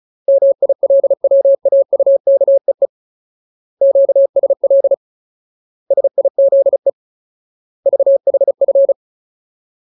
Morse code, heard as a keyed tone, about 34 wpm, 550 Hz. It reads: MILWAUKEE QSL SIZE VHF
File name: Ultimate Word List / 35WPM Ultimate